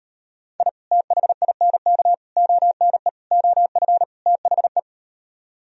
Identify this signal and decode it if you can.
Morse code, heard as a keyed tone, about 38 wpm, 700 Hz. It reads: I THINK ONE OF THE